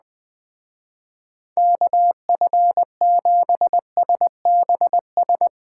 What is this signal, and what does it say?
Morse code, heard as a keyed tone, about 19 wpm, 700 Hz. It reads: KF7SBS
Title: Speed Racing - USA 2x3 Call Signs 17wpm